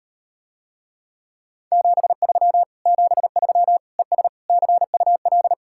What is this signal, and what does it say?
Morse code, heard as a keyed tone, about 38 wpm, 700 Hz. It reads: 73 73 ES CUL